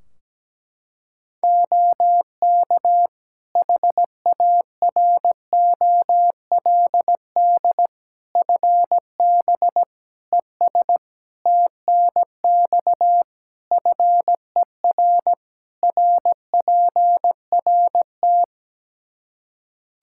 Morse code, heard as a keyed tone, about 17 wpm, 700 Hz. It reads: OK HAROLD FB ES TNX FER RPRT
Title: Practice QSO - Exchange 2 - Info - 17wpm